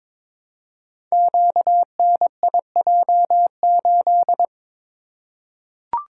Morse code, heard as a keyed tone, about 22 wpm, 700 Hz. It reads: QNIJ8
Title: Code Groups - 5 Letter:Numbers - 22wpm